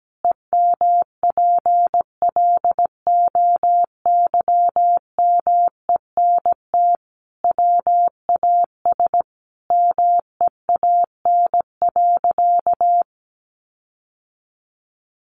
Morse code, heard as a keyed tone, about 17 wpm, 700 Hz. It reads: EMPLOYMENT WAS MEAN.
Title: Sentences from Top 1500 Words 17wpm